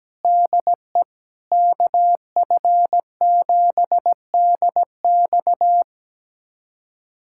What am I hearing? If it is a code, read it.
Morse code, heard as a keyed tone, about 17 wpm, 700 Hz. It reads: DE KF7DX
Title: USA Callsign DE Callsign - 5 Characters - 17wpm